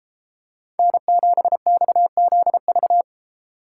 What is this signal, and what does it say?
Morse code, heard as a keyed tone, about 32 wpm, 700 Hz. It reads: N7XZV